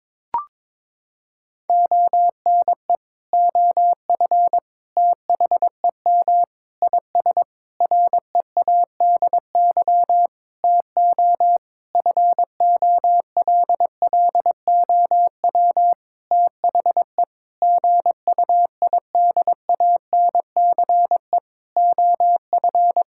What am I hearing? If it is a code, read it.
Morse code, heard as a keyed tone, about 22 wpm, 700 Hz. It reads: ONE OF THEM IS READY TO FOLLOW THE GUIDANCE OF
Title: Sentences from Top 1600 Words 22wpm